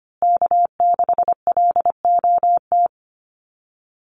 Morse code, heard as a keyed tone, about 25 wpm, 700 Hz. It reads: K6LOT
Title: USA Call Signs Repeated 2x 25wpm